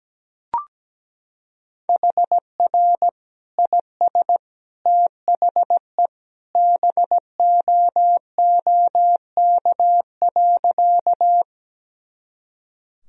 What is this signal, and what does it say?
Morse code, heard as a keyed tone, about 17 wpm, 700 Hz. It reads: HR IS THE BOOK.
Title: Sentences from Top 300 Words and Abbreviations 17wpm